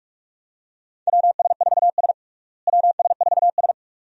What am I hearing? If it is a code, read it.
Morse code, heard as a keyed tone, about 46 wpm, 700 Hz. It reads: WS4S WS4S
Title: USA Call Signs - 4 Characters - Repeated 2x - 45wpm